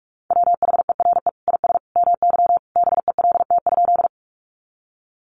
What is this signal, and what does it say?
Morse code, heard as a keyed tone, about 44 wpm, 700 Hz. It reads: WHERE IS MY BELT?
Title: Sentences from Top 1400 Words 45wpm